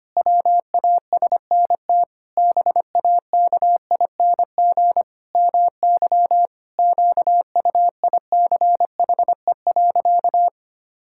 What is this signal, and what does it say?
Morse code, heard as a keyed tone, about 25 wpm, 700 Hz. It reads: WASNT BAKING MY QUICHE.
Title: Pangram Challenge v2 - 25wpm